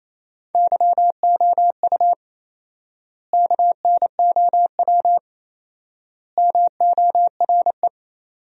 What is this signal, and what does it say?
Morse code, heard as a keyed tone, about 28 wpm, 700 Hz. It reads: YOU KNOW MORE
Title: 4x Word-Spacing - Sentences from Top 100 Words - 28wpm